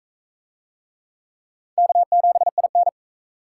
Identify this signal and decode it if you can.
Morse code, heard as a keyed tone, about 44 wpm, 700 Hz. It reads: K7IN